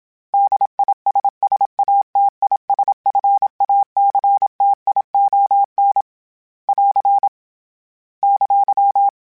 Morse code, heard as a keyed tone, about 26 wpm, 800 Hz. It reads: DISSATISFACTION <AR> !
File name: General / General-mp3s / ISS @25